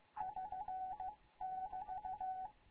Morse code, heard as a keyed tone, about 15 wpm, spaced as for 13 wpm, 750 Hz. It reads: <SN><BT>
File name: Morze-HA2MN / hangp / 2 11 GYAKORLAT